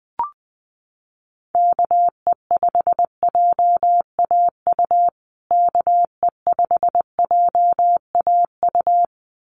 Morse code, heard as a keyed tone, about 20 wpm, 700 Hz. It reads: KE5JAU KE5JAU
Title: USA Call Signs Repeated 2x 20wpm